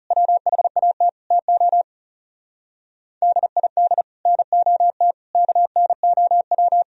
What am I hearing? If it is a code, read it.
Morse code, heard as a keyed tone, about 36 wpm, 700 Hz. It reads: WHAT TO DID NOT KNOW
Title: Speed-Racing - Sets of 5 Words 30wpm